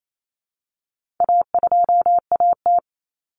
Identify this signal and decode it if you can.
Morse code, heard as a keyed tone, about 28 wpm, 700 Hz. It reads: A2AT
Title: Common Cut Numbers - 4-Digit 28wpm